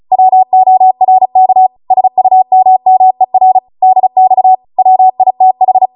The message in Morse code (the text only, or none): WORK SUMMER DX WITH